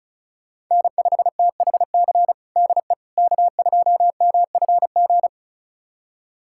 Morse code, heard as a keyed tone, about 36 wpm, 700 Hz. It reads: N5THC DE K2MFG
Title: USA Callsign DE Callsign - 5 Characters - 35wpm